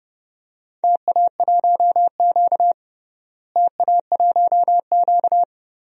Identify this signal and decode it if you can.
Morse code, heard as a keyed tone, about 30 wpm, 700 Hz. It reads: TA1Q TA1Q
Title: DXCC-100 Call Signs - 4 Characters - Repeated 2x - 30wpm